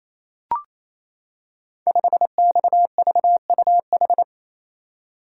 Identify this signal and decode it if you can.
Morse code, heard as a keyed tone, about 27 wpm, 700 Hz. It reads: 5XVUH
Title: Code Groups - 5 Letter:Numbers - 28wpm